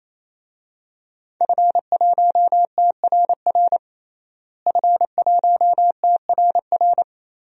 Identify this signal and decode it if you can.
Morse code, heard as a keyed tone, about 28 wpm, 700 Hz. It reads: F1TRR F1TRR
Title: DXCC-100 Call Signs - 5 Characters - Repeated 3x - 28wpm